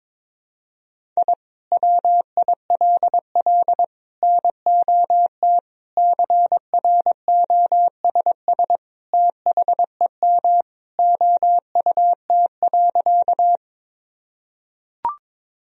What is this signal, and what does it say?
Morse code, heard as a keyed tone, about 22 wpm, 700 Hz. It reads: I WILL NOT CROSS THEM OUT.
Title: Sentences from Top 1000 Words - Review Part 1 - 22wpm